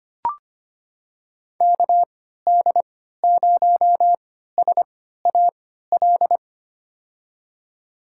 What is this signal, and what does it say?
Morse code, heard as a keyed tone, about 25 wpm, spaced as for 14 wpm, 700 Hz. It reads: KD0SAL
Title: USA Call Signs - 6 Characters - 3x Character Spacing - 25wpm